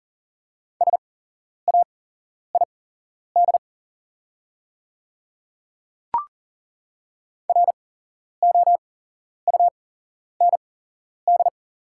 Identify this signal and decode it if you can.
Morse code, heard as a keyed tone, about 38 wpm, spaced as for 11 wpm, 690 Hz. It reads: SAID ROUND
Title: 8x Character Spacing - Top 300 Words - 40wpm